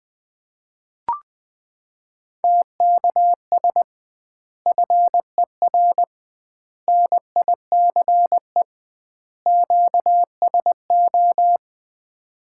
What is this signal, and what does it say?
Morse code, heard as a keyed tone, about 20 wpm, 700 Hz. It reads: TKS FER NICE QSO
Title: Common QSO Phrases - 2x Word-Spacing - 20wpm